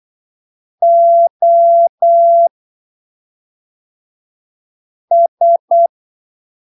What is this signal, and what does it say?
Morse code, heard as a keyed tone, about 8 wpm, spaced as for 3 wpm, 680 Hz. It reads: OS
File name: arquivos / texto / texto 25 08ppm